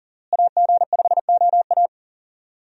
Morse code, heard as a keyed tone, about 40 wpm, 700 Hz. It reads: AG5OA
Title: USA Call Signs - 5 Characters - Repeated 2x - 40wpm